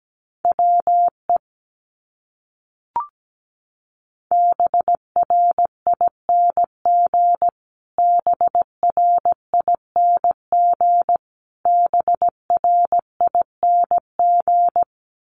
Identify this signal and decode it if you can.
Morse code, heard as a keyed tone, about 17 wpm, 700 Hz. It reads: WE BRING BRING BRING
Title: Top 500 Words Repeated 17wpm